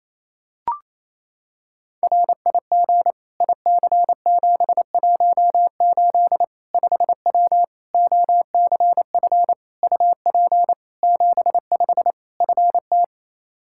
Morse code, heard as a keyed tone, about 27 wpm, 700 Hz. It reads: RIG IC718 5W OCF UP 75 FT
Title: K1USN SST QSO Practice 28wpm